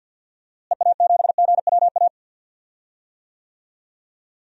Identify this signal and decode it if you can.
Morse code, heard as a keyed tone, about 50 wpm, 700 Hz. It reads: EA7GWA
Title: DXCC-100 Call Signs - 6 Characters - Repeated 3x - 50wpm